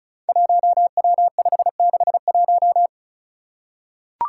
Morse code, heard as a keyed tone, about 36 wpm, 700 Hz. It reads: 1W561
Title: Five Challenging Letter-Number Combinations 35wpm